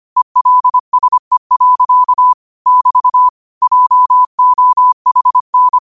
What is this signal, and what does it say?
Morse code, heard as a keyed tone, about 25 wpm, 1000 Hz. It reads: ELSE. <BT> JOHN
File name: qotdfiles / 25wpm